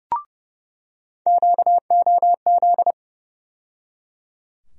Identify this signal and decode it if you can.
Morse code, heard as a keyed tone, about 32 wpm, 700 Hz. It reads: QOZ
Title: USA Call Sign Postfixes - 3 Letters 30wpm